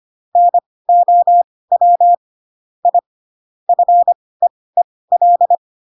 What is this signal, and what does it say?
Morse code, heard as a keyed tone, about 25 wpm, spaced as for 18 wpm, 700 Hz. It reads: NOW I FEEL
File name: Pinocchio / Pinocchio ch7 12 25 ik1hge